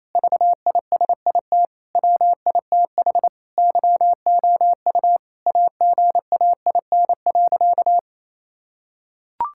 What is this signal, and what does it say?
Morse code, heard as a keyed tone, about 28 wpm, 700 Hz. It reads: VISIT WITH YOU AGAIN.
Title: Sentences from Top 1000 Words - Review Part 6 - 28wpm